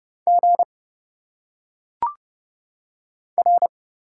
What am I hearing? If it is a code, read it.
Morse code, heard as a keyed tone, about 30 wpm, 700 Hz. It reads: G R
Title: Single Letters - Phonetic Pronunciation 30wpm